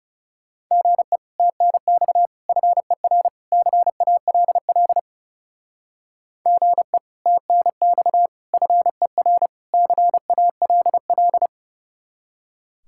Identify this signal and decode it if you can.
Morse code, heard as a keyed tone, about 37 wpm, 700 Hz. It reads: GE TNX FER CALL GE TNX FER CALL
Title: Speed-Racing Common QSO Phrases - 30wpm